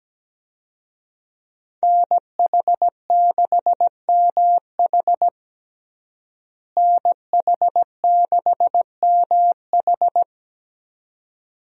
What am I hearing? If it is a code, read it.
Morse code, heard as a keyed tone, about 17 wpm, 700 Hz. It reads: NH6MH NH6MH
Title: USA Call Signs - 5 Characters - Repeated 2x - 17wpm